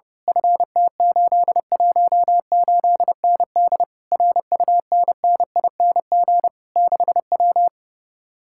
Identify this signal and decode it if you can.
Morse code, heard as a keyed tone, about 28 wpm, 700 Hz. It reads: FT818ND RUNNING 6W
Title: Speed-Racing Common QSO Phrases Encore - 30wpm